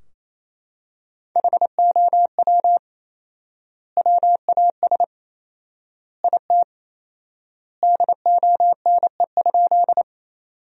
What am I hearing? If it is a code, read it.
Morse code, heard as a keyed tone, about 28 wpm, 700 Hz. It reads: HOW WAS IT DONE?
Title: 4x Word-Spacing - Sentences from Top 500 Words - 28wpm